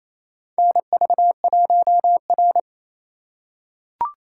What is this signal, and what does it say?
Morse code, heard as a keyed tone, about 28 wpm, 700 Hz. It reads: NV1R
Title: USA Call Signs - 4 Characters - 6x Character Spacing - 28wpm